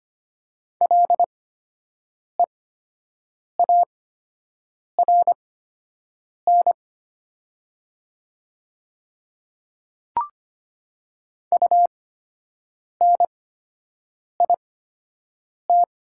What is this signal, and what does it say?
Morse code, heard as a keyed tone, about 25 wpm, spaced as for 7 wpm, 700 Hz. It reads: LEARN UNIT
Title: 8x Character Spacing - Top 500 Words - 25wpm